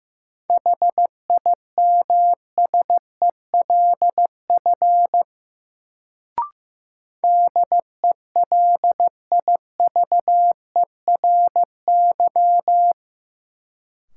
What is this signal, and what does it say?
Morse code, heard as a keyed tone, about 15 wpm, 700 Hz. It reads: HIMSELF DELIVERY
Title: Sentences from Top 1500 Words - New Word Drill - 15wpm